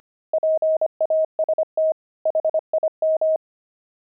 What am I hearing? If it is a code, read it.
Morse code, heard as a keyed tone, about 25 wpm, 630 Hz. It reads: PAST HIM
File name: pangram-fs-25-cs-25-axle